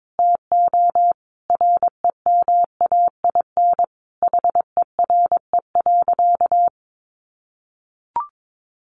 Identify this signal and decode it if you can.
Morse code, heard as a keyed tone, about 22 wpm, 700 Hz. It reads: TO REMAIN HERE.
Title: Sentences from Top 1200 Words 22wpm